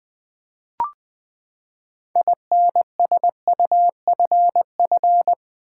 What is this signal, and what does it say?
Morse code, heard as a keyed tone, about 20 wpm, 700 Hz. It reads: INSUFF
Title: Build A Word - 6 Letters At A Time - 12-Letter Words - 20wpm